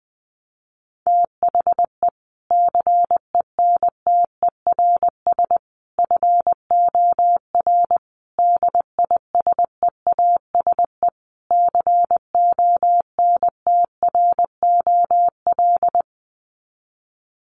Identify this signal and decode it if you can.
Morse code, heard as a keyed tone, about 20 wpm, 700 Hz. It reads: THE CENTERS FOR DISEASE CONTROL
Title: Sets of 5 Words Encore2 20wpm